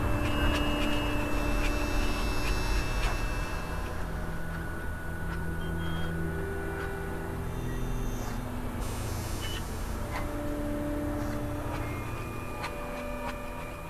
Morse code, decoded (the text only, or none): none